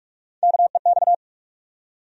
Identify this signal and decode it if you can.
Morse code, heard as a keyed tone, about 46 wpm, 700 Hz. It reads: KEX
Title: USA Call Sign Postfixes - 3 Letters 45wpm